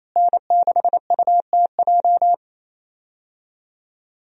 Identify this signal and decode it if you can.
Morse code, heard as a keyed tone, about 28 wpm, 700 Hz. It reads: N6UTJ
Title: USA Call Signs - 5 Characters - Repeated 2x - 28wpm